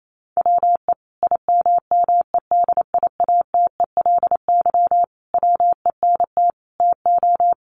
Morse code, heard as a keyed tone, about 28 wpm, 700 Hz. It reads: WE IMMEDIATELY WENT TO
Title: Sentences from Top 1100 Words 28wpm